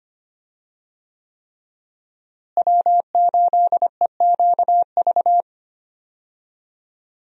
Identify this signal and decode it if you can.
Morse code, heard as a keyed tone, about 25 wpm, 700 Hz. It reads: W8EQV
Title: USA Call Signs - 5 Characters - Repeated 3x - 25wpm